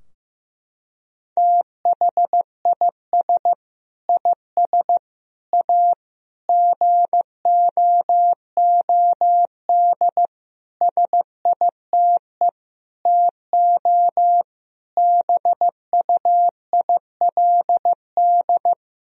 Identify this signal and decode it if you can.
Morse code, heard as a keyed tone, about 15 wpm, 700 Hz. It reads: THIS IS A GOOD SITE TO BUILD
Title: Sentences from Top 1000 Words - Review Part 4 - 15wpm